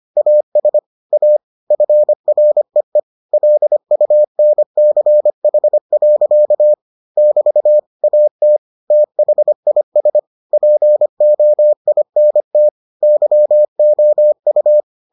Morse code, heard as a keyed tone, about 25 wpm, 600 Hz. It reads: AS A FREE LUNCH. <BT> AT THIS POINT YOU